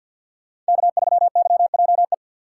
Morse code, heard as a keyed tone, about 47 wpm, 700 Hz. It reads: K3YJE
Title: USA Call Signs - 5 Characters - 4x Character Spacing - 50wpm